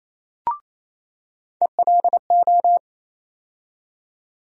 Morse code, heard as a keyed tone, about 28 wpm, 700 Hz. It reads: ELO
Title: USA Call Sign Postfixes - 3 Letters 28wpm